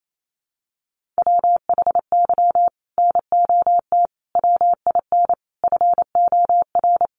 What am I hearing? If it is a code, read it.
Morse code, heard as a keyed tone, about 28 wpm, 700 Hz. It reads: WHY NOT WIN FOR
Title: Sentences from Top 1500 Words 28wpm